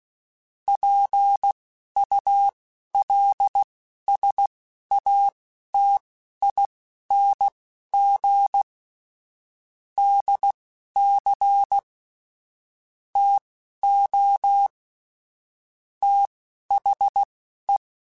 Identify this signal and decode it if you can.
Morse code, heard as a keyed tone, about 16 wpm, spaced as for 11 wpm, 800 Hz. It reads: PULSATING DC TO THE
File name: Technical Subjects / Technical Subjects-mp3s / Sw Pwr Sup @10